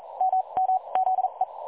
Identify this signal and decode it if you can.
Morse code, heard as a keyed tone, about 42 wpm, spaced as for 32 wpm, 750 Hz. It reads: MM ZE
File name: dlazaawansowanych / nagraniazpasma / websdr recording 3565.0kHz